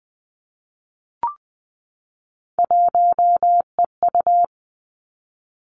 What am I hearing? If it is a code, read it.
Morse code, heard as a keyed tone, about 20 wpm, 700 Hz. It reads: 1EU
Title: Three Challenging Letter-Number Combinations 20wpm